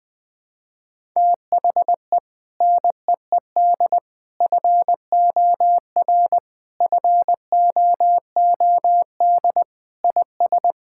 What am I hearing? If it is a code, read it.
Morse code, heard as a keyed tone, about 20 wpm, 700 Hz. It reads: THE NEED FOR FOOD IS
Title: Sentences with 3-Word Phrases - 20wpm